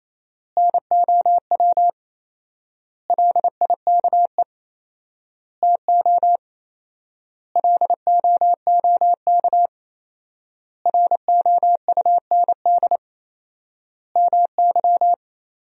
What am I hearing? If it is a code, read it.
Morse code, heard as a keyed tone, about 27 wpm, 700 Hz. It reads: NOW LIKE TO LOOK ROUND MY
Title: 4x Word-Spacing - Sentences from Top 200 Words - 28wpm